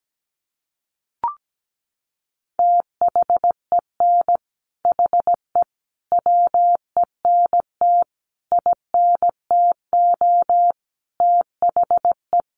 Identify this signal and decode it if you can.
Morse code, heard as a keyed tone, about 17 wpm, 700 Hz. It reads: THEN HE WENT INTO THE
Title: Sentences from Top 1500 Words 17wpm